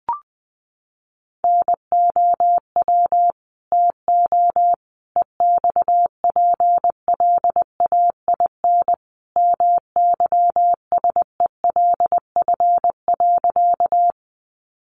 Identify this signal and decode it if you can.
Morse code, heard as a keyed tone, about 20 wpm, 700 Hz. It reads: NOW TO EXPLAIN MYSELF.
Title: Sentences from Top 1300 Words 20wpm